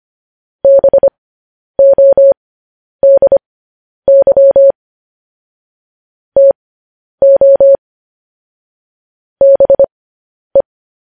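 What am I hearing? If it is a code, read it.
Morse code, heard as a keyed tone, about 25 wpm, spaced as for 10 wpm, 550 Hz. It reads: BODY TO BE
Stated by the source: Morse code news headlines sent at 10 wpm for practice.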